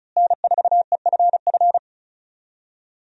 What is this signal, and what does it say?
Morse code, heard as a keyed tone, about 34 wpm, 700 Hz. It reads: N4EFF
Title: USA Call Signs - 5 Characters - Repeated 3x - 35wpm